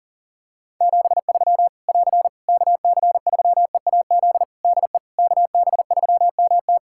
Morse code, heard as a keyed tone, about 40 wpm, 700 Hz. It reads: 73 <AR> KC3EAZ DE KB3MT
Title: Practice QSO - Exchange 4 - Ending - 40wpm